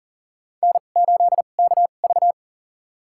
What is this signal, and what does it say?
Morse code, heard as a keyed tone, about 40 wpm, spaced as for 29 wpm, 700 Hz. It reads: N8KV
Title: USA Call Signs - 4 Characters - 2x Character Spacing - 40wpm